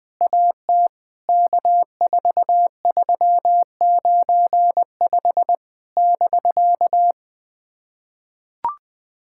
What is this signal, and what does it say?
Morse code, heard as a keyed tone, about 20 wpm, 700 Hz. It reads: AT K4395 <BK>
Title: POTA QSO Practice - 20wpm